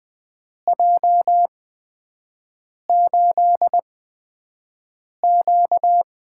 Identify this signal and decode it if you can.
Morse code, heard as a keyed tone, about 20 wpm, 700 Hz. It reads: J 8 Q
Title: USA Call Signs - 4 Characters - 8x Character Spacing - 20wpm